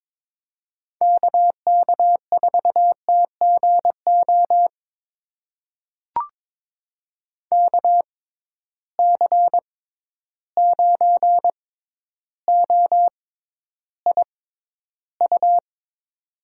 Morse code, heard as a keyed tone, about 22 wpm, spaced as for 11 wpm, 700 Hz. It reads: KK4TGO KC9OIU
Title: USA Call Signs - 6 Characters - 6x Character Spacing - 22wpm